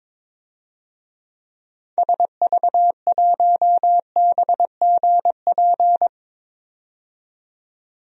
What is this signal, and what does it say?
Morse code, heard as a keyed tone, about 22 wpm, 700 Hz. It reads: SV1BGP
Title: DXCC-100 Call Signs - 6 Characters - 22wpm